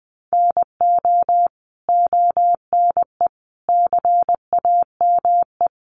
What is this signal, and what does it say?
Morse code, heard as a keyed tone, about 20 wpm, 700 Hz. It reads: NO ONE CAME
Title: Sentences from Top 1300 Words 20wpm